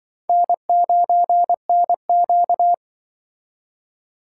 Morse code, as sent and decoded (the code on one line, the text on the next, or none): -. ----. -. --.-
N9NQ